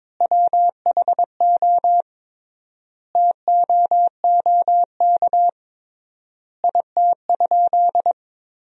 Morse code, heard as a keyed tone, about 22 wpm, 700 Hz. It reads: WHO TOOK IT?
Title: 3x Word-Spacing - Sentences from Top 300 Words - 22wpm